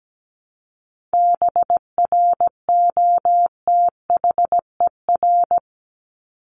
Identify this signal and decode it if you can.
Morse code, heard as a keyed tone, about 17 wpm, 700 Hz. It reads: BROTHER
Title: Top 1751-2000 Words 17wpm